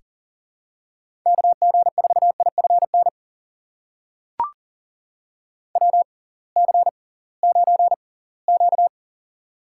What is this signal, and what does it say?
Morse code, heard as a keyed tone, about 39 wpm, spaced as for 23 wpm, 700 Hz. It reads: KG4IFN WC9Q
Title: USA Call Signs - 6x Character Spacing - 40wpm